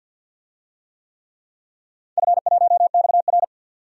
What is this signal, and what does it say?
Morse code, heard as a keyed tone, about 52 wpm, 700 Hz. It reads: F1XR
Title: DXCC-100 Call Signs - 4 Characters - 50wpm